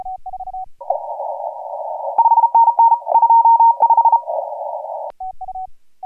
Morse code, heard as a keyed tone, about 33 wpm, 920 Hz. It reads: 5NN 25
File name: AUDIO2018 / HG1G CW